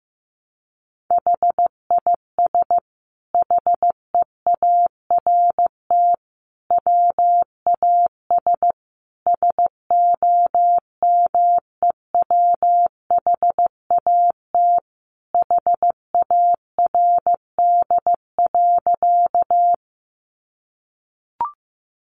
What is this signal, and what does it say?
Morse code, heard as a keyed tone, about 15 wpm, 700 Hz. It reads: HIS HEART WAS SOMEWHAT HARD.
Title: Sentences from Top 1300 Words 15wpm